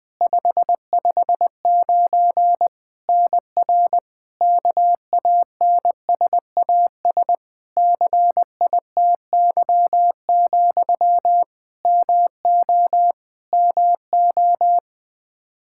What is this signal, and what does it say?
Morse code, heard as a keyed tone, about 20 wpm, 700 Hz. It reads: 559 NR KANSAS CITY, MO MO
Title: K1USN SST QSO Practice 20wpm